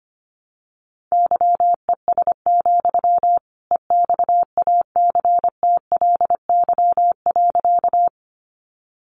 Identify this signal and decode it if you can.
Morse code, heard as a keyed tone, about 25 wpm, 700 Hz. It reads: YES, EXACTLY.
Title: Sentences from Top 900 Words 25wpm